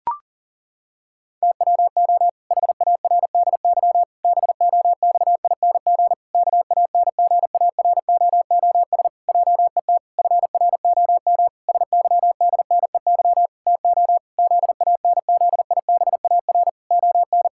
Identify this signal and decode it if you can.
Morse code, heard as a keyed tone, about 40 wpm, 700 Hz. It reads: TWO HARDY BOXING KANGAROOS JET FROM SYDNEY TO ZANZIBAR ON
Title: Pangram Challenge v2 - 40wpm